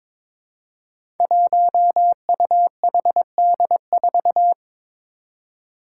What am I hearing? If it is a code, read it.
Morse code, heard as a keyed tone, about 22 wpm, 700 Hz. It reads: 1UHD4
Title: Five Challenging Letter-Number Combinations 22wpm